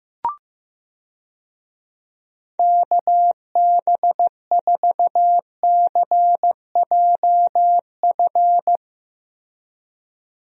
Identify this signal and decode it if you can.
Morse code, heard as a keyed tone, about 15 wpm, 700 Hz. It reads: KB4CJF
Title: USA Call Signs - 6 Characters - Repeated 2x - 15wpm